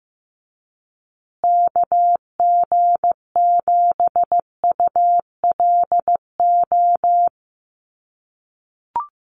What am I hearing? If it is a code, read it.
Morse code, heard as a keyed tone, about 15 wpm, 700 Hz. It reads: KG7ULO
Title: USA Call Signs Repeated 3x 15wpm